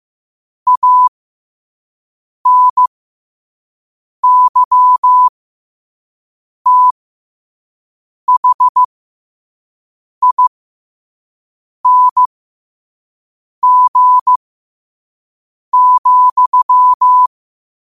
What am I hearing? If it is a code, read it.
Morse code, heard as a keyed tone, about 15 wpm, spaced as for 5 wpm, 1000 Hz. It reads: ANYTHING,
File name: qotdfiles / 5wpm